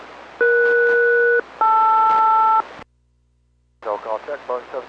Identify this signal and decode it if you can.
Aircraft SELCAL, heard as two pairs of tones, with speech. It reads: ES-LR